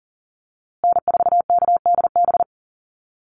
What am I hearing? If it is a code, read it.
Morse code, heard as a keyed tone, about 39 wpm, 700 Hz. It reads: N4KDB
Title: USA Call Signs Repeated 2x 40wpm